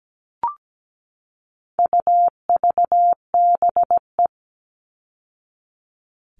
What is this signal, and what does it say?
Morse code, heard as a keyed tone, about 17 wpm, 700 Hz. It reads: UVBE
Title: All Cut Numbers - 4-Digit 17wpm